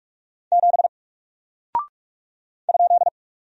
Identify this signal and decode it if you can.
Morse code, heard as a keyed tone, about 45 wpm, 700 Hz. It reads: 7 ?
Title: Single Character (Letter:Number:Punctuation:Prosign) 45wpm